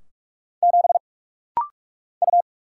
Morse code, heard as a keyed tone, about 45 wpm, 700 Hz. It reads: 7 U
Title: Single Letter-Number - Rapid-Fire 45wpm